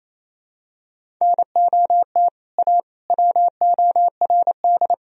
Morse code, heard as a keyed tone, about 28 wpm, 700 Hz. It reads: NOT A WORD
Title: Sentences from Top 1000 Words - Review Part 1 - 28wpm